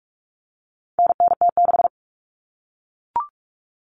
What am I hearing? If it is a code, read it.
Morse code, heard as a keyed tone, about 44 wpm, 700 Hz. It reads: NNT6
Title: Common Cut Numbers - 4-Digit 45wpm